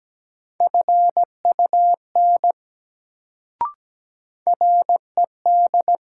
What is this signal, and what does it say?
Morse code, heard as a keyed tone, about 17 wpm, 700 Hz. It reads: FUN RED
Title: Sentences from Top 700 Words - New Word Drill - 17wpm